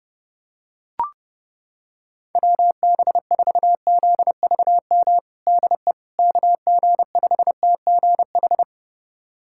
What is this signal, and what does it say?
Morse code, heard as a keyed tone, about 30 wpm, 700 Hz. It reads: WB4ZVM DE KG5TGH
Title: USA Callsign DE Callsign - 6 Characters - 30wpm